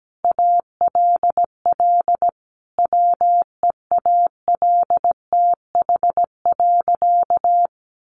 Morse code, heard as a keyed tone, about 17 wpm, 700 Hz. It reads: ALL WEALTH.
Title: Sentences from Top 1200 Words 17wpm